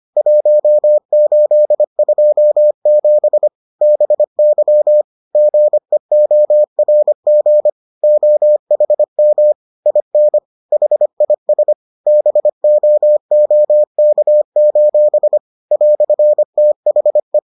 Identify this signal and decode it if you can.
Morse code, heard as a keyed tone, about 25 wpm, 600 Hz. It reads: 1827 BY GEORG OHM IN HIS BOOK: "THE